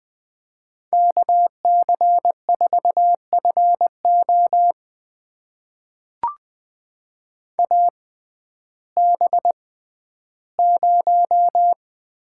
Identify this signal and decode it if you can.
Morse code, heard as a keyed tone, about 20 wpm, spaced as for 12 wpm, 700 Hz. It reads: KC4FO AB0
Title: USA Call Signs - 5 Characters - 6x Character Spacing - 20wpm